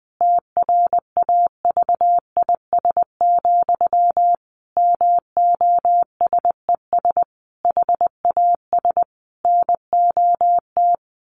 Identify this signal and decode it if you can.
Morse code, heard as a keyed tone, about 20 wpm, 700 Hz. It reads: TRAVIS, MOSES HAS NOT
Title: A Sound Of Thunder 20wpm